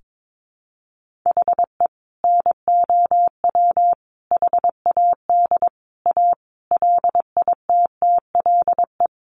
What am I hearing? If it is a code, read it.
Morse code, heard as a keyed tone, about 22 wpm, 700 Hz. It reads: HE NOW HAD A LITTLE
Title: Sentences from Top 1500 Words 22wpm